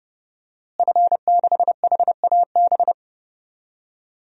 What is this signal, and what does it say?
Morse code, heard as a keyed tone, about 30 wpm, 700 Hz. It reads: F6HAB
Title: DXCC-100 Call Signs - 5 Characters - Repeated 3x - 30wpm